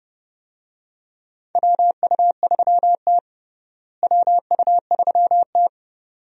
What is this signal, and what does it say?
Morse code, heard as a keyed tone, about 32 wpm, 700 Hz. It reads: WU3T WU3T
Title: USA Call Signs - 4 Characters - Repeated 2x - 30wpm